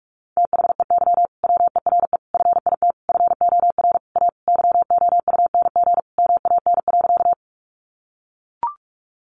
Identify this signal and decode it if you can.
Morse code, heard as a keyed tone, about 44 wpm, 700 Hz. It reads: THEY WERE FIT FOR A YOUNG MAN.
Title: Sentences from Top 800 Words 45wpm